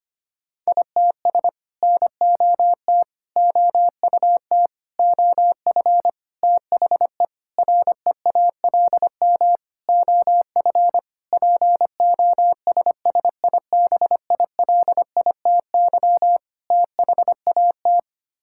Morse code, heard as a keyed tone, about 25 wpm, 700 Hz. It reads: ITS NOT OUT OF THE REALM OF POSSIBILITY THAT
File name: Sets of 9 Words - 25wpm